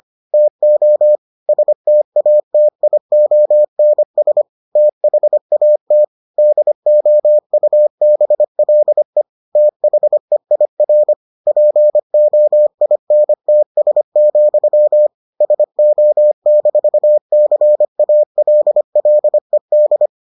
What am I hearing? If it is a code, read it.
Morse code, heard as a keyed tone, about 25 wpm, 600 Hz. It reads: TO STATIONS THAT DOUBLE THEIR POINTS, SO-CALLED